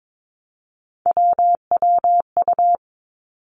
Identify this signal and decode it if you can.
Morse code, heard as a keyed tone, about 22 wpm, 700 Hz. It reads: WWU
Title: Three Challenging Letter-Number Combinations 22wpm